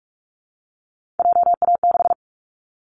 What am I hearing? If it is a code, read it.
Morse code, heard as a keyed tone, about 44 wpm, 700 Hz. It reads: JA6
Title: Callsign Prefixes for Top 50 DXCC - 45wpm